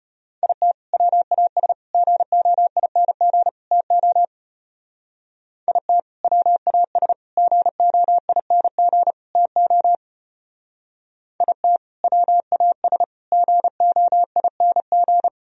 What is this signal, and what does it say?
Morse code, heard as a keyed tone, about 30 wpm, 700 Hz. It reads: IT WAS GOING TO IT WAS GOING TO IT WAS GOING
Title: Speed-Racing - Sets of 4 Words 25wpm